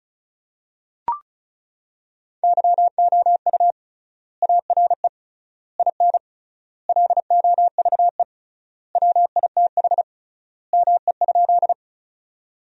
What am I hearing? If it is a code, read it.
Morse code, heard as a keyed tone, about 36 wpm, 700 Hz. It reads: YOU ARE IN LOVE WITH ME?
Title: 3x Word-Spacing - Sentences from Top 400 Words - 35wpm